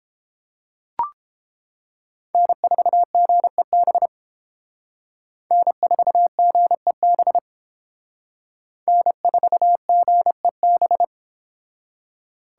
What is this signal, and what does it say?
Morse code, heard as a keyed tone, about 30 wpm, 700 Hz. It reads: N4GEB N4GEB N4GEB